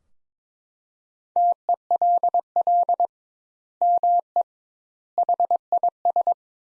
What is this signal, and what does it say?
Morse code, heard as a keyed tone, about 22 wpm, 700 Hz. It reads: TELL ME HIS
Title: Sentences Top 200 Words - 22wpm